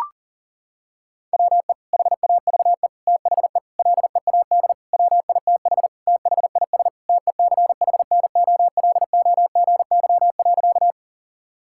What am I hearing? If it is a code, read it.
Morse code, heard as a keyed tone, about 40 wpm, 700 Hz. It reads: WE HAVE THE LEAD WITH THIS TECHNOLOGY.
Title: Sentences from Top 1000 Words - Review Part 5 - 40wpm